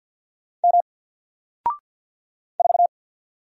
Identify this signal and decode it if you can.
Morse code, heard as a keyed tone, about 50 wpm, 700 Hz. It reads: M 4
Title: Single Character (Letter:Number:Punctuation:Prosign) 50wpm